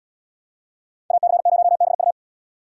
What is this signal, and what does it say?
Morse code, heard as a keyed tone, about 69 wpm, 700 Hz. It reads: AC0BU